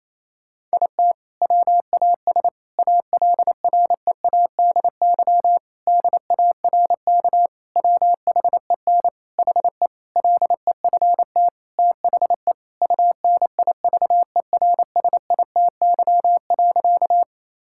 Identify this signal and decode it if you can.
Morse code, heard as a keyed tone, about 28 wpm, 700 Hz. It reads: IT WAS ALREADY DARK WHEN HE LEFT THE UNIVERSITY.
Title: Sentences from Top 1000 Words - Review Part 5 - 28wpm